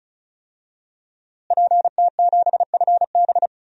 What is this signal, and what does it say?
Morse code, heard as a keyed tone, about 35 wpm, 700 Hz. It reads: PT7FB
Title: DXCC-100 Call Signs - 5 Characters - Repeated 2x - 35wpm